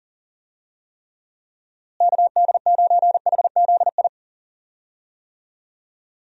Morse code, heard as a keyed tone, about 40 wpm, 700 Hz. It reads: KD9HZI
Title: USA Call Signs - 6 Characters - Repeated 2x - 40wpm